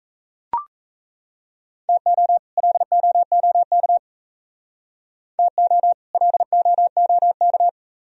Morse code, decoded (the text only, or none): TO LOOK TO LOOK